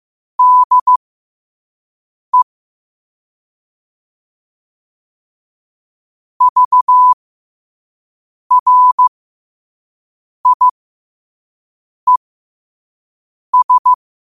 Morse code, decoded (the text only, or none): DE VRIES